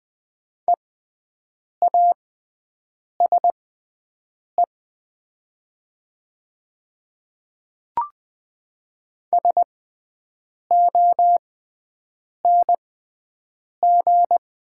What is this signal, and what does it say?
Morse code, heard as a keyed tone, about 20 wpm, spaced as for 7 wpm, 700 Hz. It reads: EASE SONG
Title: 6x Character Spacing - Top 500 Words - 20wpm